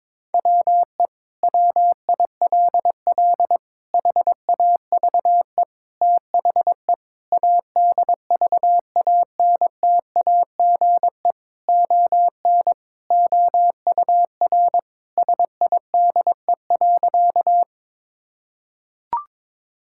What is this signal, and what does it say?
Morse code, heard as a keyed tone, about 22 wpm, 700 Hz. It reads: WE WILL HAVE THE ADVANTAGE ON OUR SIDE.
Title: Sentences from Top 1000 Words - Review Part 6 - 22wpm